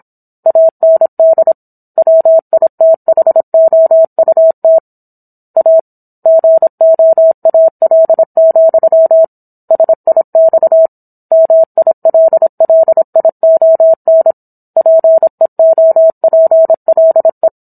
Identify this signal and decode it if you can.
Morse code, heard as a keyed tone, about 27 wpm, 650 Hz. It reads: AND WITHOUT A GOAL, SIX MILLION PEOPLE
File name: Book One / 17-Chapter 17 - 26 WPM